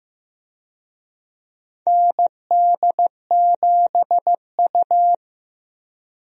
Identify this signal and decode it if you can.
Morse code, heard as a keyed tone, about 15 wpm, 700 Hz. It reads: ND7U
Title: USA Call Signs - 4 Characters - Repeated 3x - 15wpm